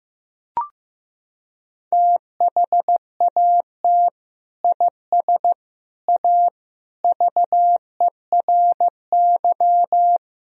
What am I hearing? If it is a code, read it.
Morse code, heard as a keyed tone, about 15 wpm, 700 Hz. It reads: THAT IS A VERY
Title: Sentences from Top 1000 Words - Review Part 1 - 15wpm